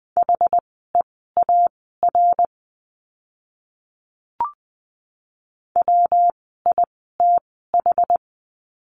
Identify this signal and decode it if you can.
Morse code, heard as a keyed tone, about 20 wpm, spaced as for 15 wpm, 700 Hz. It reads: HEAR WITH
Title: 206 Lesson 33f - Letter j - Words 20wpm